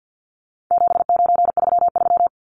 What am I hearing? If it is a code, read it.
Morse code, heard as a keyed tone, about 50 wpm, 700 Hz. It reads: 7933
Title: Numbers - 4-Digit 50wpm